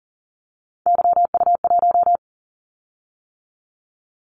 Morse code, heard as a keyed tone, about 40 wpm, 700 Hz. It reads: YU1
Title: Callsign Prefixes for Top 25 DXCC - 40wpm